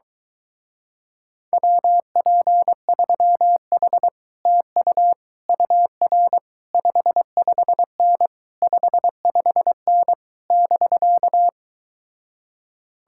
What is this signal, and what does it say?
Morse code, heard as a keyed tone, about 24 wpm, 700 Hz. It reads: WP3H TU UR 55N 55N <BK>